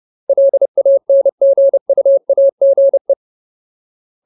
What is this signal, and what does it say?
Morse code, heard as a keyed tone, about 30 wpm, 550 Hz. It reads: LANGUAGE
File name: Ultimate Word List / 30WPM Ultimate